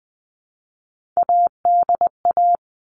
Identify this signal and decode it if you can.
Morse code, heard as a keyed tone, about 20 wpm, 700 Hz. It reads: ADA
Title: All Cut Numbers - 3-Digit 20wpm